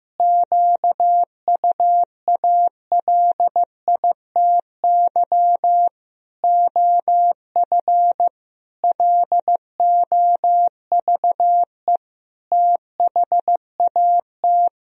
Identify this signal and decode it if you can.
Morse code, heard as a keyed tone, about 15 wpm, 700 Hz. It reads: QUALITY OF LOVE THAT
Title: Sentences from Top 1000 Words - Review Part 4 - 15wpm